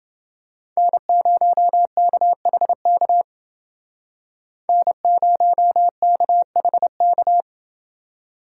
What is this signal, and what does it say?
Morse code, heard as a keyed tone, about 29 wpm, 700 Hz. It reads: N0KHK N0KHK